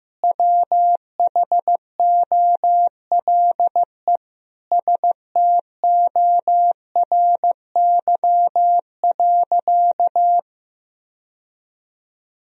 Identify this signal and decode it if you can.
Morse code, heard as a keyed tone, about 15 wpm, 700 Hz. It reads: WHOLE STORY.
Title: Sentences from Top 1000 Words - Review Part 3 - 15wpm